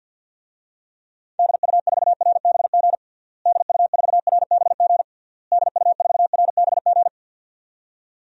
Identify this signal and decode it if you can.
Morse code, heard as a keyed tone, about 51 wpm, 700 Hz. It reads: DU4RBG DU4RBG DU4RBG
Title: DXCC-100 Call Signs - 6 Characters - Repeated 3x - 50wpm